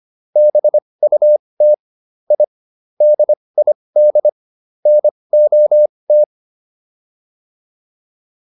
Morse code, heard as a keyed tone, about 25 wpm, spaced as for 20 wpm, 600 Hz. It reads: BUT I DID NOT